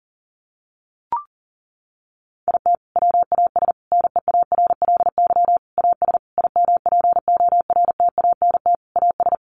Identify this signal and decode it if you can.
Morse code, heard as a keyed tone, about 40 wpm, 700 Hz. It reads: IT WAS NEARLY AS IMPORTANT AS
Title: Sentences from Top 1400 Words 40wpm